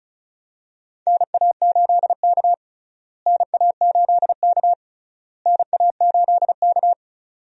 Morse code, heard as a keyed tone, about 36 wpm, 700 Hz. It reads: NA8K NA8K NA8K
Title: USA Call Signs - 4 Characters - Repeated 3x - 35wpm